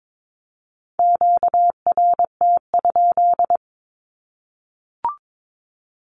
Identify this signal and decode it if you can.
Morse code, heard as a keyed tone, about 22 wpm, 700 Hz. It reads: QRT?
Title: Q-Codes Repeated 3x 22wpm